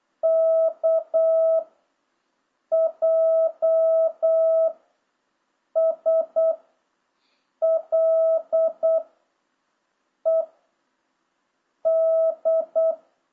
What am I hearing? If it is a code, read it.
Morse code, heard as a keyed tone, about 8 wpm, 660 Hz. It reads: K J S L E D